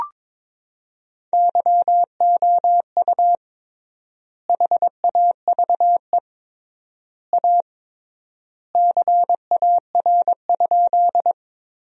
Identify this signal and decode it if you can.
Morse code, heard as a keyed tone, about 22 wpm, 700 Hz. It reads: YOU HAVE A CAR?
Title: 3x Word-Spacing - Sentences from Top 300 Words - 22wpm